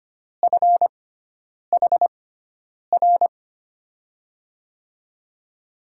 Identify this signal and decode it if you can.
Morse code, heard as a keyed tone, about 25 wpm, spaced as for 9 wpm, 700 Hz. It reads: FHR
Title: USA Call Signs - 6x Character Spacing - 25wpm